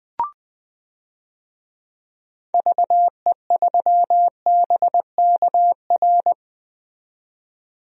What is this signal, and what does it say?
Morse code, heard as a keyed tone, about 20 wpm, 700 Hz. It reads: VE3BKR
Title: DXCC-100 Call Signs - 6 Characters - 20wpm